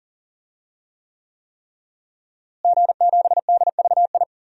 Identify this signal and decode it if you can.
Morse code, heard as a keyed tone, about 40 wpm, 700 Hz. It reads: G7DVI
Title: DXCC-100 Call Signs - 5 Characters - Repeated 3x - 40wpm